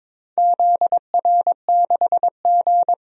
Morse code, heard as a keyed tone, about 22 wpm, 700 Hz. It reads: ZR6G
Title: DXCC-100 Call Signs - 4 Characters - 22wpm